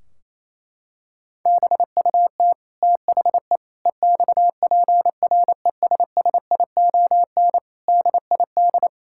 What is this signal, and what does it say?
Morse code, heard as a keyed tone, about 27 wpm, 700 Hz. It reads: BUT THE EXPRESSION DID
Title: Sentences from Top 1600 Words 28wpm